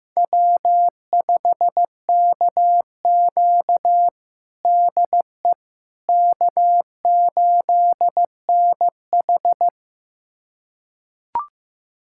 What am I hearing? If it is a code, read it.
Morse code, heard as a keyed tone, about 15 wpm, 700 Hz. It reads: W5KQ DE K8NH
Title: USA Callsign DE Callsign - 4 Characters - 15wpm